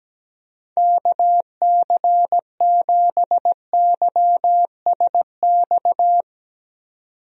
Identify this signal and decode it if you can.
Morse code, heard as a keyed tone, about 17 wpm, 700 Hz. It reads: KC7YSX
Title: USA Call Signs - 6 Characters - 17wpm